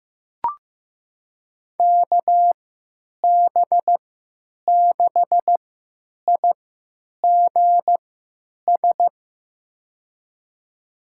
Morse code, heard as a keyed tone, about 15 wpm, spaced as for 9 wpm, 700 Hz. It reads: KB6IGS
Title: USA Call Signs - 6 Characters - 3x Character Spacing - 15wpm